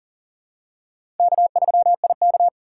Morse code, heard as a keyed tone, about 40 wpm, 700 Hz. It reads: K3IK
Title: USA Call Signs - 4 Characters - 6x Character Spacing - 40wpm